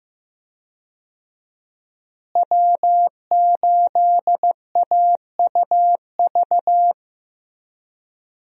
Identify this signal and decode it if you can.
Morse code, heard as a keyed tone, about 15 wpm, 700 Hz. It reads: W8AUV
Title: USA Call Signs - 5 Characters - Repeated 2x - 15wpm